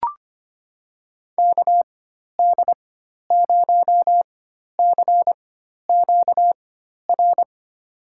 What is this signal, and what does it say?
Morse code, heard as a keyed tone, about 25 wpm, spaced as for 12 wpm, 700 Hz. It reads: KD0CQR
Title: USA Call Signs - 6 Characters - 4x Character Spacing - 25wpm